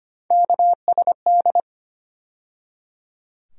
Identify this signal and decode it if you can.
Morse code, heard as a keyed tone, about 25 wpm, 700 Hz. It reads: KSD
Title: USA Call Sign Postfixes - 3 Letters 25wpm